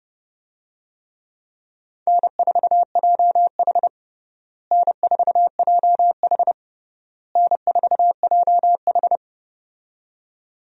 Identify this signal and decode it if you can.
Morse code, heard as a keyed tone, about 32 wpm, 700 Hz. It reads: N4JH N4JH N4JH
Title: USA Call Signs - 4 Characters - Repeated 3x - 30wpm